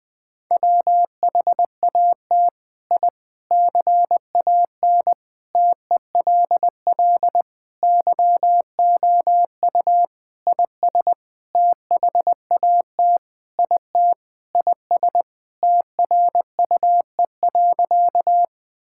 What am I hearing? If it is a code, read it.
Morse code, heard as a keyed tone, about 20 wpm, 700 Hz. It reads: WHAT I CAN TELL YOU IS THAT IT IS TRUE.
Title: Sentences with 7-Word Phrases - Spoken Beginning - 20wpm